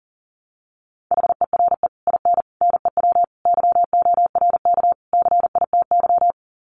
Morse code, heard as a keyed tone, about 40 wpm, 700 Hz. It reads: HERE IN NEW YORK CITY
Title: Sets of 5 Words Encore2 40wpm